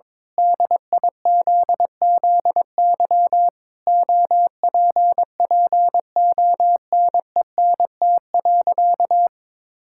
Morse code, heard as a keyed tone, about 22 wpm, 700 Hz. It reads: DIZZY OPPONENT.
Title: Pangram Challenge v2 - 22wpm